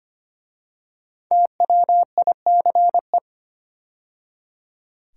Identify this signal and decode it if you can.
Morse code, heard as a keyed tone, about 25 wpm, 700 Hz. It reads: TWICE
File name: 5 Letter Words - First 3 Letters Given - 25wpm